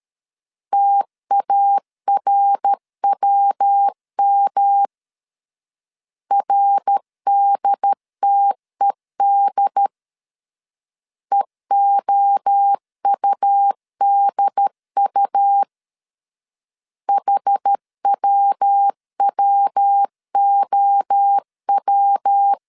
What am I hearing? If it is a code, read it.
Morse code, heard as a keyed tone, about 13 wpm, 800 Hz. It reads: TARWM RDTED EOUDU HWWOW
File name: CW / DET / 1 3 1500C S B 12wpm 2828